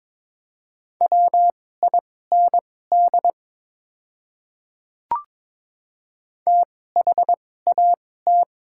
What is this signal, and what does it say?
Morse code, heard as a keyed tone, about 22 wpm, spaced as for 16 wpm, 700 Hz. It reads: WIND THAT
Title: 2x Character Spacing - Top 500 Words - 22wpm